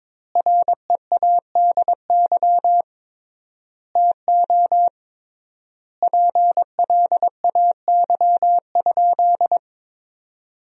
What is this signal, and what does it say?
Morse code, heard as a keyed tone, about 22 wpm, 700 Hz. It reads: READY TO PLAY?
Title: 3x Word-Spacing - Sentences from Top 400 Words - 22wpm